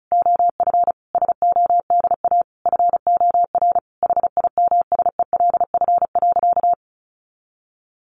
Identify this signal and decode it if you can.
Morse code, heard as a keyed tone, about 34 wpm, 700 Hz. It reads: OF SODA FOR HIMSELF.
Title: Additional Easy Phrases 35wpm